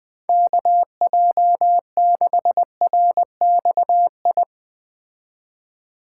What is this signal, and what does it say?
Morse code, heard as a keyed tone, about 20 wpm, 700 Hz. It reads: KJ6RXI
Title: Speed Racing - USA 2x3 Call Signs 17wpm